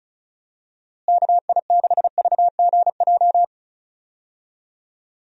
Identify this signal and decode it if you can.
Morse code, heard as a keyed tone, about 34 wpm, 700 Hz. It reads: KI6VGJ
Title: USA Call Signs - 6 Characters - Repeated 2x - 35wpm